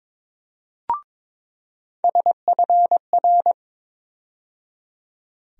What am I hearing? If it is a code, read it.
Morse code, heard as a keyed tone, about 22 wpm, 700 Hz. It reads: SFR
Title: USA Call Sign Postfixes - 3 Letters 22wpm